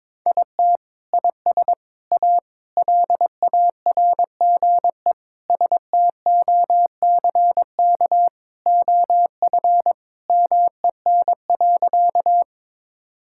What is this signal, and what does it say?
Morse code, heard as a keyed tone, about 22 wpm, 700 Hz. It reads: IT IS A LARGE STOCK OF MEN.
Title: Sentences from Top 1000 Words - Review Part 5 - 22wpm